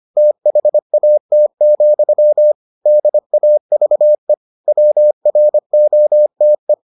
Morse code, heard as a keyed tone, about 25 wpm, 600 Hz. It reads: THAT, DAVE WROTE